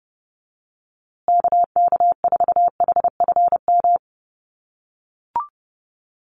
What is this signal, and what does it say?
Morse code, heard as a keyed tone, about 30 wpm, 700 Hz. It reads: KK4HFM
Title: USA Call Signs Repeated 3x 30wpm